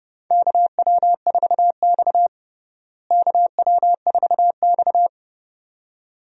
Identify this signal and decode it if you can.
Morse code, heard as a keyed tone, about 30 wpm, 700 Hz. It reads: KW4X KW4X
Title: USA Call Signs - 4 Characters - Repeated 2x - 30wpm